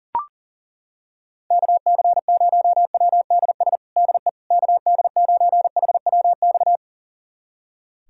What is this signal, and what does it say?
Morse code, heard as a keyed tone, about 40 wpm, 700 Hz. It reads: KC0WDS DE KD9HWX
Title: USA Callsign DE Callsign - 6 Characters - 40wpm